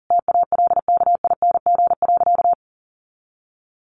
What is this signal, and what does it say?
Morse code, heard as a keyed tone, about 37 wpm, 700 Hz. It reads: TALKING.
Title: Additional Easy Phrases 40wpm